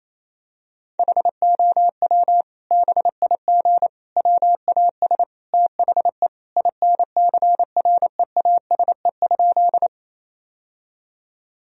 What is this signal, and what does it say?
Morse code, heard as a keyed tone, about 28 wpm, 700 Hz. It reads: HOW BIG WAS THE INCREASE?
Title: Sentences from Top 1000 Words - Review Part 4 - 28wpm